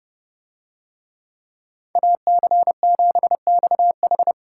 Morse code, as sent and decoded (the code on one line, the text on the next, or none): .- -.-. --... -..- ....
AC7XH